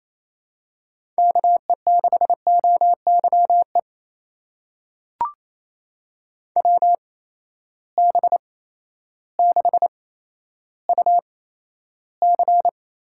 Morse code, heard as a keyed tone, about 28 wpm, spaced as for 13 wpm, 700 Hz. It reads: KE6OYE WB6UC
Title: USA Call Signs - 6 Characters - 8x Character Spacing - 28wpm